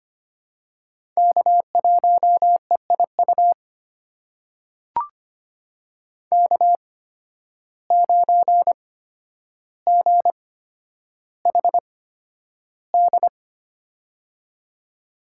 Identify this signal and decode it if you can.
Morse code, heard as a keyed tone, about 25 wpm, spaced as for 11 wpm, 700 Hz. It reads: K1EIU K9GHD
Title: USA Call Signs - 5 Characters - 8x Character Spacing - 25wpm